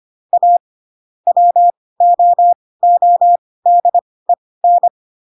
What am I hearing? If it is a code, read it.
Morse code, heard as a keyed tone, about 25 wpm, spaced as for 18 wpm, 700 Hz. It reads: A WOODEN
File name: Pinocchio ch3 12 25 ik1hge